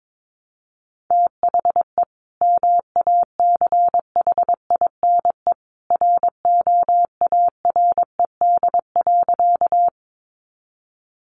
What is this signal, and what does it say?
Morse code, heard as a keyed tone, about 22 wpm, 700 Hz. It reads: THE MACHINE ROARED.
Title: A Sound Of Thunder 22wpm